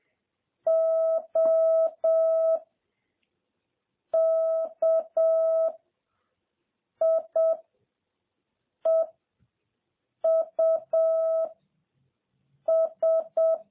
Morse code, heard as a keyed tone, about 7 wpm, 650 Hz. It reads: O K I E U S
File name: jh8cbh / 2021cw